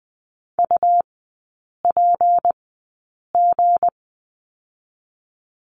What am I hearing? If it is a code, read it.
Morse code, heard as a keyed tone, about 20 wpm, spaced as for 8 wpm, 690 Hz. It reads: UPG
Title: 198 Lesson 32e - Letter k - Three Characters 20wpm